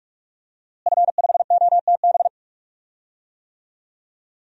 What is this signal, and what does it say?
Morse code, heard as a keyed tone, about 42 wpm, 700 Hz. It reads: F5OTB
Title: DXCC-100 Call Signs - 5 Characters - Repeated 3x - 45wpm